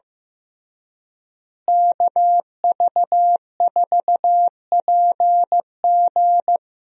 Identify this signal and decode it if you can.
Morse code, heard as a keyed tone, about 15 wpm, 700 Hz. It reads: KV4PG
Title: USA Call Signs - 5 Characters - Repeated 3x - 15wpm